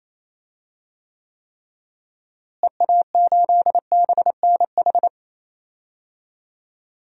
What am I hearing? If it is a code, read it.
Morse code, heard as a keyed tone, about 28 wpm, 700 Hz. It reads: EA8BNH
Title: DXCC-100 Call Signs - 6 Characters - Repeated 3x - 28wpm